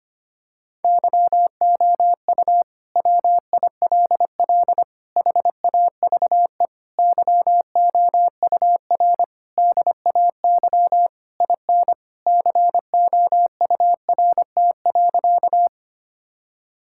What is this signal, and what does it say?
Morse code, heard as a keyed tone, about 25 wpm, 700 Hz. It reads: YOU WILL HAVE YOUR DAY IN COURT.
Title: Sentences from Top 1600 Words 25wpm